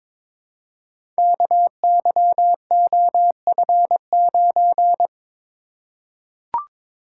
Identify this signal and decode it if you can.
Morse code, heard as a keyed tone, about 22 wpm, 700 Hz. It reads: KYOF9
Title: Code Groups - 5 Letter:Numbers - 22wpm